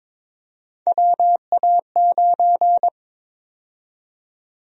Morse code, heard as a keyed tone, about 22 wpm, 700 Hz. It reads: WA9
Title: USA Call Sign Prefixes - 22wpm